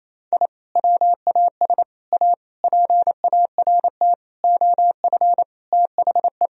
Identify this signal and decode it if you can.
Morse code, heard as a keyed tone, about 28 wpm, 700 Hz. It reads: I WAS A PART OF THE
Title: Sentences from Top 1000 Words - Review Part 1 - 28wpm